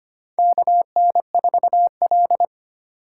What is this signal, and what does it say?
Morse code, heard as a keyed tone, about 25 wpm, 700 Hz. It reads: KN4L
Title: USA Call Signs - 4 Characters - 3x Character Spacing - 25wpm